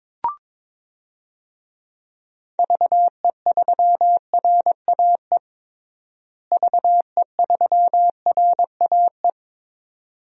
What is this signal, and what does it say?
Morse code, heard as a keyed tone, about 22 wpm, 700 Hz. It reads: VE3RAE VE3RAE
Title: DXCC-100 Call Signs - 6 Characters - Repeated 2x - 22wpm